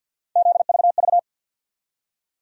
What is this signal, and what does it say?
Morse code, heard as a keyed tone, about 49 wpm, 700 Hz. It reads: ZVV
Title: USA Call Sign Postfixes - 3 Letters 50wpm